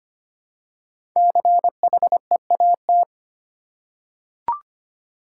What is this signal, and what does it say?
Morse code, heard as a keyed tone, about 25 wpm, 700 Hz. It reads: CHEAT
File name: 5 Letter Words - First 3 Letters Given - 25wpm